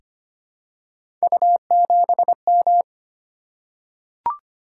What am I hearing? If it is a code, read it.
Morse code, heard as a keyed tone, about 25 wpm, 700 Hz. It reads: U7M
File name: Three Letter-Number Combinations - 25wpm